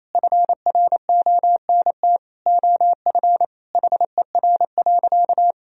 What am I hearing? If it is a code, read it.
Morse code, heard as a keyed tone, about 28 wpm, 700 Hz. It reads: FRONT OF HER.
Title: Sentences from Top 1000 Words - Review Part 2 - 28wpm